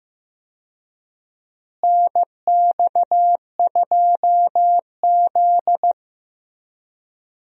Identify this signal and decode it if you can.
Morse code, heard as a keyed tone, about 15 wpm, 700 Hz. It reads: NX2Z
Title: USA Call Signs - 4 Characters - Repeated 3x - 15wpm